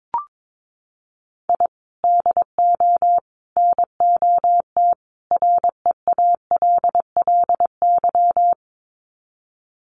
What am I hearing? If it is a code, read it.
Morse code, heard as a keyed tone, about 22 wpm, 700 Hz. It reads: I DO NOT REALLY
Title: Sets of 4 Words 22wpm